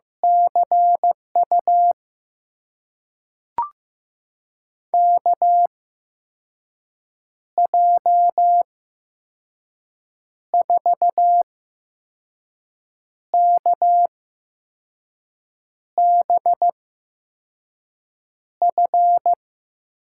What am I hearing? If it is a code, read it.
Morse code, heard as a keyed tone, about 15 wpm, spaced as for 5 wpm, 700 Hz. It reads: CU KJ4KBF
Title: USA Call Signs - 6 Characters - 8x Character Spacing - 15wpm